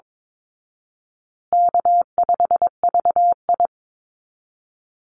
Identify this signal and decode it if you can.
Morse code, heard as a keyed tone, about 22 wpm, 700 Hz. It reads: K5VI
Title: Call Signs Repeated 2x - 22wpm